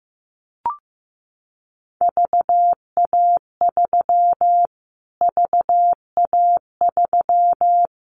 Morse code, heard as a keyed tone, about 15 wpm, 700 Hz. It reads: VA3 VA3
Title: Callsign Prefixes for Top 25 DXCC - Repeated 3x - 15wpm